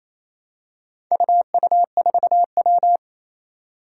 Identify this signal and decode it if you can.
Morse code, heard as a keyed tone, about 28 wpm, 700 Hz. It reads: UU4W
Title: Four Challenging Letter-Number Combinations 28wpm